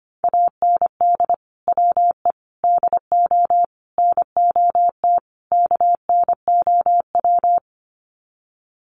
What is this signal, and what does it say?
Morse code, heard as a keyed tone, about 25 wpm, 700 Hz. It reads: AND WE DO NOT KNOW
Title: Sets of 5 Words Encore 25wpm